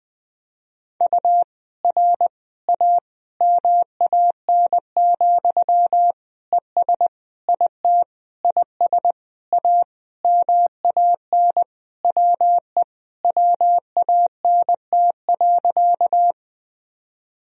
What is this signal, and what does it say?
Morse code, heard as a keyed tone, about 20 wpm, 700 Hz. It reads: U R A MAN, ES IT IS A MAN WE WANT.
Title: Sentences from Top 200 Words and Abbreviations 20wpm